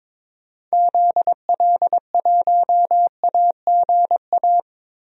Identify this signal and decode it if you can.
Morse code, heard as a keyed tone, about 21 wpm, 700 Hz. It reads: ZL1AGA
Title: DXCC-100 Call Signs - 6 Characters - Repeated 3x - 22wpm